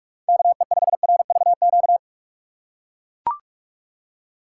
Morse code, heard as a keyed tone, about 42 wpm, 700 Hz. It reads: KE5RVQ
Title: USA Call Signs - 6 Characters - 4x Character Spacing - 45wpm